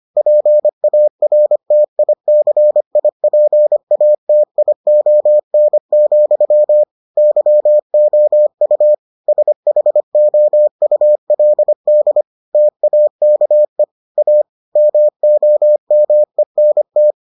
Morse code, heard as a keyed tone, about 25 wpm, 600 Hz. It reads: PARTICIPATION, YOU SHOULD TAKE A MOMENT